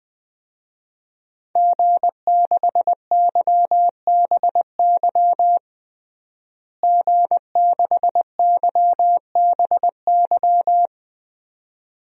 Morse code, heard as a keyed tone, about 20 wpm, 700 Hz. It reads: G6YBY G6YBY
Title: DXCC-100 Call Signs - 5 Characters - Repeated 2x - 20wpm